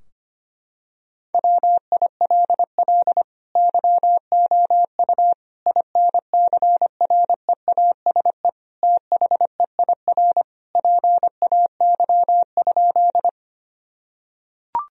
Morse code, heard as a keyed tone, about 25 wpm, 700 Hz. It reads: WILL YOU INCREASE THEIR PAY?
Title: Sentences from Top 1000 Words - Review Part 4 - 25wpm